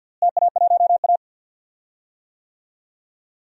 Morse code, heard as a keyed tone, about 49 wpm, 700 Hz. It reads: TA1A
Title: DXCC-100 Call Signs - 4 Characters - Repeated 3x - 50wpm